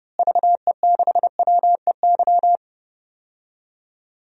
Morse code, heard as a keyed tone, about 30 wpm, 700 Hz. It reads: VE6WEY
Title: DXCC-100 Call Signs - 6 Characters - Repeated 2x - 30wpm